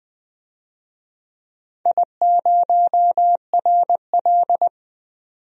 Morse code, heard as a keyed tone, about 21 wpm, 700 Hz. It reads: I0RL
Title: DXCC-100 Call Signs - 4 Characters - 20wpm